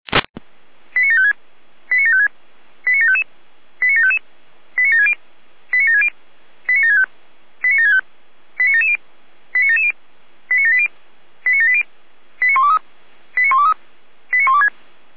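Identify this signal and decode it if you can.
Five-tone selective call (ZVEI), heard as quick runs of five tones, tones of about 80 ms. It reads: 89756 89756 89755 89755 89760 89760 89765 89765 89770 89770 89780 89780 89123 89123 89126
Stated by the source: ZVEI five-tone call received via SDR.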